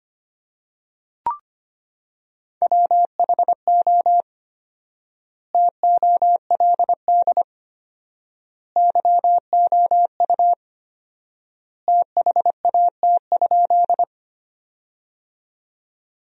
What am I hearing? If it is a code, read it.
Morse code, heard as a keyed tone, about 25 wpm, 700 Hz. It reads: WHO TOLD YOU THAT?
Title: 4x Word-Spacing - Sentences from Top 400 Words - 25wpm